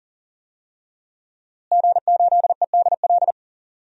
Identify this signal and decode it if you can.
Morse code, heard as a keyed tone, about 39 wpm, 700 Hz. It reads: G8EDL
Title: DXCC-100 Call Signs - 5 Characters - Repeated 2x - 40wpm